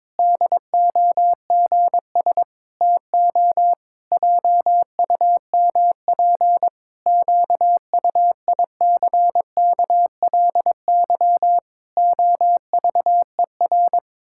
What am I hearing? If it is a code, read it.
Morse code, heard as a keyed tone, about 22 wpm, 700 Hz. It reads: DOGS TO JUMP QUICKLY OVER
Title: Pangram Challenge v2 - 22wpm